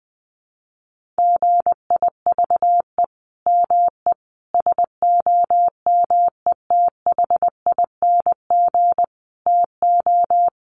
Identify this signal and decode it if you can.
Morse code, heard as a keyed tone, about 20 wpm, 700 Hz. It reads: GIVE ME SOMETHING TO
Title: Sentences from Top 1500 Words 20wpm